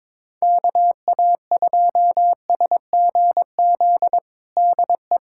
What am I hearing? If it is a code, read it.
Morse code, heard as a keyed tone, about 22 wpm, 700 Hz. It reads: KA2SGZ DE
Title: USA Callsign DE Callsign 22wpm